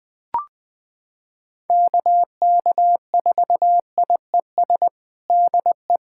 Morse code, heard as a keyed tone, about 21 wpm, 700 Hz. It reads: KK4IES DE
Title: USA Callsign DE Callsign 20wpm